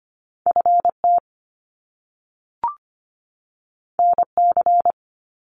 Morse code, heard as a keyed tone, about 25 wpm, 700 Hz. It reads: FT NC
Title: Top Two Letter Combinations in Words 25wpm